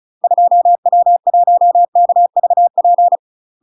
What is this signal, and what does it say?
Morse code, heard as a keyed tone, about 34 wpm, 700 Hz. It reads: 2W1KVP
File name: UK MASTER QSOs TRAINING / 35WPM / UK MASTER QSOs 0009